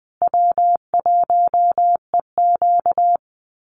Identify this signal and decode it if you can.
Morse code, heard as a keyed tone, about 20 wpm, 700 Hz. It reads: W1EQ
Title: Call Signs Repeated 3x - 20wpm